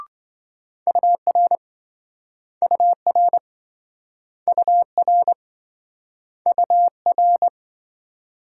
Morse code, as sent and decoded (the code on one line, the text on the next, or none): ..- .-. / ..- .-. / ..- .-. / ..- .-.
UR UR UR UR